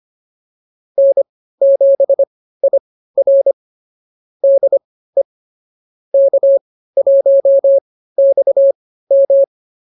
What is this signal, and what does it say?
Morse code, heard as a keyed tone, about 24 wpm, spaced as for 15 wpm, 550 Hz. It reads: N7IR DE K1XM